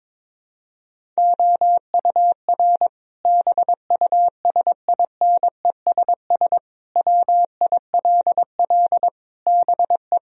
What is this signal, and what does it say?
Morse code, heard as a keyed tone, about 22 wpm, 700 Hz. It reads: OUR BUSINESS WILL BE
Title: Sentences from Top 1000 Words - Review Part 4 - 22wpm